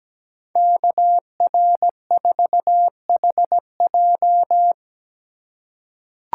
Morse code, heard as a keyed tone, about 17 wpm, 700 Hz. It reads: KR4HJ
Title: USA Call Signs - 5 Characters - 3x Character Spacing - 17wpm